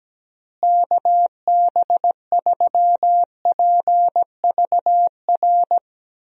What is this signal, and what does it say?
Morse code, heard as a keyed tone, about 17 wpm, 700 Hz. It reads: KB3PVR
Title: USA Call Signs - 6 Characters - Repeated 2x - 17wpm